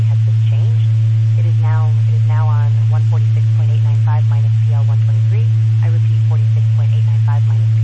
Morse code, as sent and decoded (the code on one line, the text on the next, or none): none